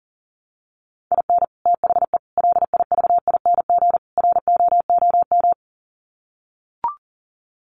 Morse code, heard as a keyed tone, about 40 wpm, 700 Hz. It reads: IN THE LIVING ROOM
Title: Sets of 4 Words 40wpm